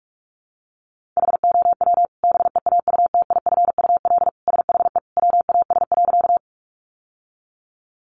Morse code, heard as a keyed tone, about 44 wpm, 700 Hz. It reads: HOW BEAUTIFUL SHE WAS.
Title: Sentences from Top 900 Words 45wpm